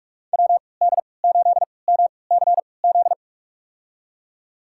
Morse code, heard as a keyed tone, about 42 wpm, spaced as for 25 wpm, 700 Hz. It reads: WD8MCZ
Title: USA Call Signs - 6 Characters - 3x Character Spacing - 45wpm